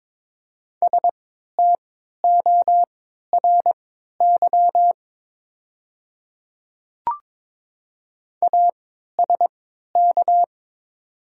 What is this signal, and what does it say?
Morse code, heard as a keyed tone, about 22 wpm, spaced as for 12 wpm, 700 Hz. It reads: STORY ASK
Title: 3x Character Spacing - Top 300 Words - 22wpm